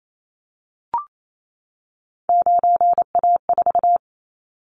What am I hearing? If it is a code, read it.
Morse code, heard as a keyed tone, about 28 wpm, 700 Hz. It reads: 9A4
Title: Callsign Prefixes for Top 50 DXCC - 28wpm